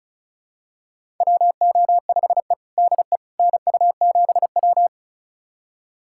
Morse code, heard as a keyed tone, about 36 wpm, 700 Hz. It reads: WO5E DE NU7W
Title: USA Callsign DE Callsign - 4 Characters - 35wpm